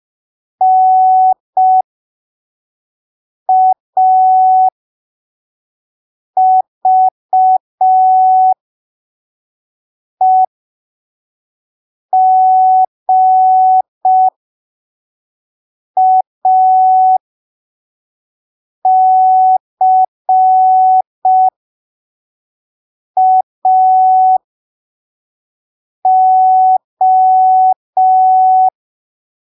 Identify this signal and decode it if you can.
Morse code, heard as a keyed tone, about 5 wpm, spaced as for 3 wpm, 750 Hz. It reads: NAVEGACAO
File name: arquivos / texto / texto 03 5ppm